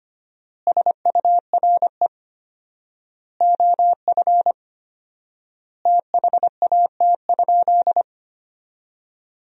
Morse code, heard as a keyed tone, about 25 wpm, 700 Hz. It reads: SURE OF THAT?
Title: 4x Word-Spacing - Sentences from Top 300 Words - 25wpm